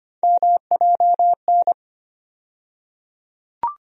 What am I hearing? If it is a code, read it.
Morse code, heard as a keyed tone, about 25 wpm, 700 Hz. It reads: MJN
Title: USA Call Sign Postfixes - 3 Letters 25wpm